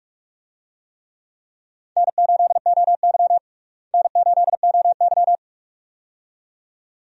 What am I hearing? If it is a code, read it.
Morse code, heard as a keyed tone, about 42 wpm, 700 Hz. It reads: N8OY N8OY
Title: USA Call Signs - 4 Characters - Repeated 2x - 45wpm